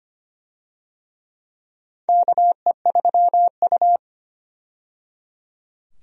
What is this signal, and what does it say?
Morse code, heard as a keyed tone, about 25 wpm, 700 Hz. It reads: KE3U
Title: USA Call Signs - 4 Characters - 25wpm